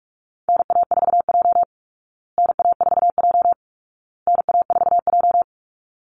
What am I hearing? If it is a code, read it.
Morse code, heard as a keyed tone, about 44 wpm, 700 Hz. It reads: NA4J NA4J NA4J
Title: Call Signs Repeated 2x - 45wpm